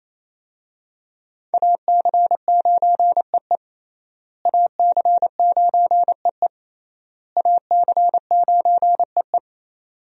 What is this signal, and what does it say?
Morse code, heard as a keyed tone, about 28 wpm, 700 Hz. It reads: AC9EE AC9EE AC9EE
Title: USA Call Signs - 5 Characters - Repeated 3x - 28wpm